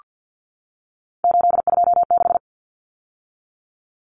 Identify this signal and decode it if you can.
Morse code, heard as a keyed tone, about 48 wpm, 700 Hz. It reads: 826
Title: Common Cut Numbers - 3-Digit 50wpm